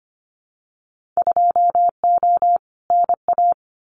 Morse code, heard as a keyed tone, about 25 wpm, 700 Hz. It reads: 2O NA
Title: Winter Field Day Exchanges 25wpm